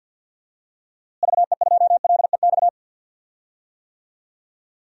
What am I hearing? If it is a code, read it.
Morse code, heard as a keyed tone, about 53 wpm, 700 Hz. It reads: VE2LEX
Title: DXCC-100 Call Signs - 6 Characters - Repeated 2x - 50wpm